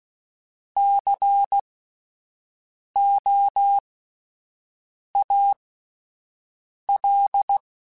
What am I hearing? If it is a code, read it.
Morse code, heard as a keyed tone, about 16 wpm, spaced as for 5 wpm, 800 Hz. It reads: COAL
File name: Alphabetical / Alphabetical mp3s / Cs @5